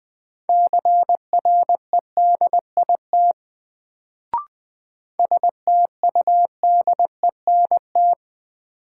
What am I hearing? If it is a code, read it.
Morse code, heard as a keyed tone, about 20 wpm, 700 Hz. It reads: CREDIT STUDENT
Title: Sentences from Top 1000 Words - New Word Drill - 20wpm